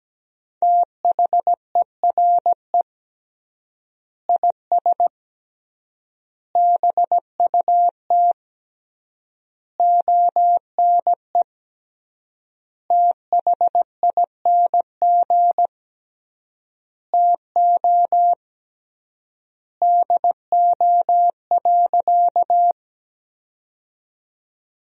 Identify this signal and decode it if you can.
Morse code, heard as a keyed tone, about 17 wpm, 700 Hz. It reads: THERE IS BUT ONE THING TO DO.
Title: 3x Word-Spacing - Sentences from Top 100 Words - 17wpm